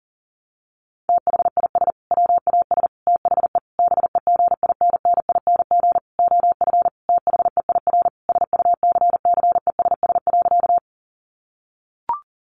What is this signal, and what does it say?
Morse code, heard as a keyed tone, about 40 wpm, 700 Hz. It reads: THIS WAS THE BEGINNING OF THEIR SUCCESS.
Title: Sentences from Top 900 Words 40wpm